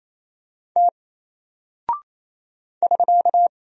Morse code, heard as a keyed tone, about 27 wpm, 700 Hz. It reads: T <SK>
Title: Single Character (Letter:Number:Punctuation:Prosign) 28wpm